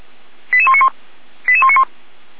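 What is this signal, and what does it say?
Five-tone selective call (ZVEI), heard as quick runs of five tones, tones of about 70 ms. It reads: 88191 88191
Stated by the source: ZVEI five-tone selective call received by SDR, from audio server of feuerwehr-orlamuende.org (fire brigade Orlamünde).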